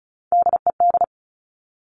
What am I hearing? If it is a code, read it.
Morse code, heard as a keyed tone, about 36 wpm, 700 Hz. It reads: DED
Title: Top Three Letter Combinations in Words 35wpm